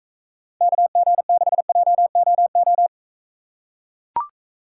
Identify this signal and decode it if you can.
Morse code, heard as a keyed tone, about 43 wpm, 700 Hz. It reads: KG6JOO